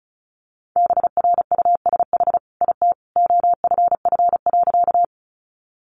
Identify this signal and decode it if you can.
Morse code, heard as a keyed tone, about 34 wpm, 700 Hz. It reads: BRUSH IT OFF.
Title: Sentences from Top 1100 Words 35wpm